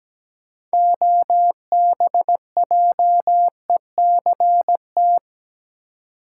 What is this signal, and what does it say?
Morse code, heard as a keyed tone, about 17 wpm, 700 Hz. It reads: OBJECT
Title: Six Letter Words - 17wpm